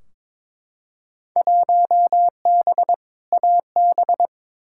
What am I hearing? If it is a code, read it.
Morse code, heard as a keyed tone, about 22 wpm, 700 Hz. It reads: 1B AB
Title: Field Day Exchanges - 22wpm